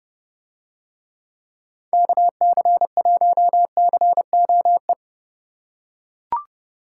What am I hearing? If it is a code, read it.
Morse code, heard as a keyed tone, about 29 wpm, 700 Hz. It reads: KC1COE